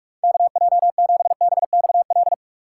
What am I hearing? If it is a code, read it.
Morse code, heard as a keyed tone, about 67 wpm, spaced as for 55 wpm, 700 Hz. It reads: KJ7BXL
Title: Speed Racing - USA 2x3 Call Signs 45wpm